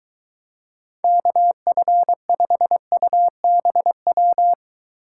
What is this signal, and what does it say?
Morse code, heard as a keyed tone, about 23 wpm, 700 Hz. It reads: KF5UBW